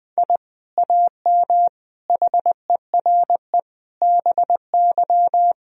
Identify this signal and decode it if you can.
Morse code, heard as a keyed tone, about 20 wpm, 700 Hz. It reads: I AM HERE BY
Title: Sentences from Top 1000 Words - Review Part 3 - 20wpm